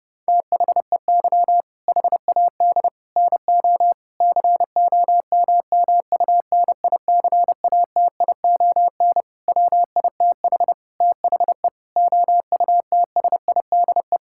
Morse code, heard as a keyed tone, about 32 wpm, 700 Hz. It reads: THEY HAD NO COMMUNICATION WITH THE OUTSIDE
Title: Sentences from Top 1000 Words - Review Part 6 - 30wpm